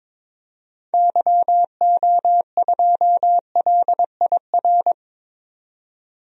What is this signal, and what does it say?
Morse code, heard as a keyed tone, about 22 wpm, 700 Hz. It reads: YO2LIR
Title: DXCC-100 Call Signs - 6 Characters - 22wpm